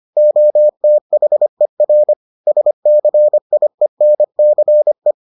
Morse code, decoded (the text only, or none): OTHER SCIENCE